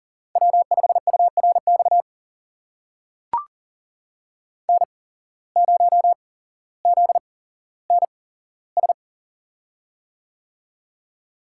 Morse code, heard as a keyed tone, about 40 wpm, spaced as for 17 wpm, 700 Hz. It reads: W5URX N0ZNS
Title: USA Call Signs - 5 Characters - 8x Character Spacing - 40wpm